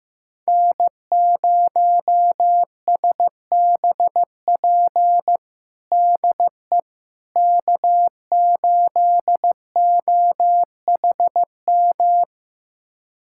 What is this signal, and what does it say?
Morse code, heard as a keyed tone, about 15 wpm, 700 Hz. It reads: N0SBP DE K8OHM
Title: USA Callsign DE Callsign - 5 Characters - 15wpm